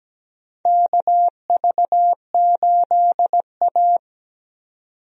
Morse code, heard as a keyed tone, about 17 wpm, 700 Hz. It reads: KV8A
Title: USA Call Signs - 4 Characters - 17wpm